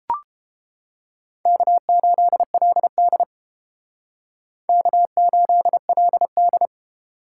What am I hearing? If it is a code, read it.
Morse code, heard as a keyed tone, about 31 wpm, 700 Hz. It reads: K8LD K8LD